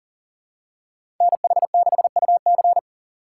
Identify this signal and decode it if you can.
Morse code, heard as a keyed tone, about 40 wpm, 700 Hz. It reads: NH6UC
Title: USA Call Signs - 5 Characters - 2x Character Spacing - 40wpm